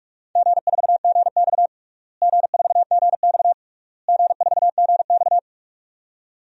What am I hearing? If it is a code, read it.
Morse code, heard as a keyed tone, about 46 wpm, 700 Hz. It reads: G4GX G4GX G4GX
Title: DXCC-100 Call Signs - 4 Characters - Repeated 3x - 45wpm